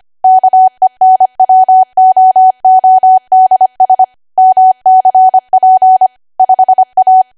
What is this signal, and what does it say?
Morse code, heard as a keyed tone, about 25 wpm, 750 Hz. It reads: KENWOODS MCP 5A
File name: Archive / 25 WPM / 25WPM